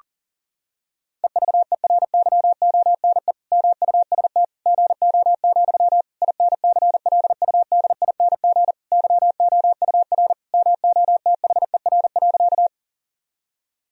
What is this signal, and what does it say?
Morse code, heard as a keyed tone, about 40 wpm, 700 Hz. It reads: EVERYONE MUST GO, INCLUDING YOUR MOTHER.
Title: Sentences from Top 1000 Words - Review Part 4 - 40wpm